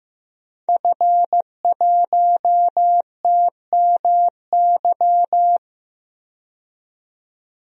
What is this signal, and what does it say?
Morse code, heard as a keyed tone, about 15 wpm, 700 Hz. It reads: F1TMY
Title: DXCC-100 Call Signs - 5 Characters - 15wpm